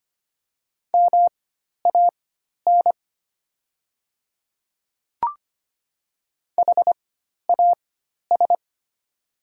Morse code, heard as a keyed tone, about 25 wpm, spaced as for 12 wpm, 690 Hz. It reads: MAN HAS
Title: 4x Character Spacing - Top 500 Words - 25wpm